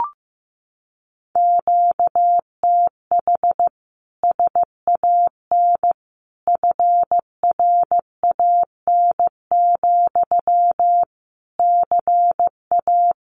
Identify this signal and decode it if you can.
Morse code, heard as a keyed tone, about 15 wpm, 700 Hz. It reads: QTH SAN FRAN, CA
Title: Common QSO Phrases Encore 15wpm